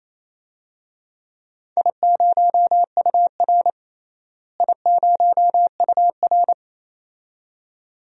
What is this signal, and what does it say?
Morse code, heard as a keyed tone, about 28 wpm, 700 Hz. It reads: I0UR I0UR
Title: DXCC-100 Call Signs - 4 Characters - Repeated 2x - 28wpm